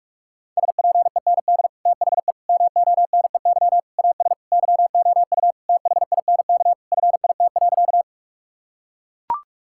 Morse code, heard as a keyed tone, about 41 wpm, 700 Hz. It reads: SPEND THE MONEY AS YOU THINK FIT.
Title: Sentences from Top 1000 Words - Review Part 5 - 45wpm